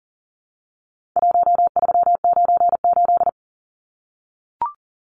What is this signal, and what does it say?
Morse code, heard as a keyed tone, about 40 wpm, 700 Hz. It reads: 1398
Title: Numbers - Contest Style - 4-Digit 40wpm